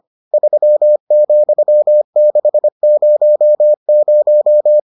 Morse code, heard as a keyed tone, about 25 wpm, 600 Hz. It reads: 3,600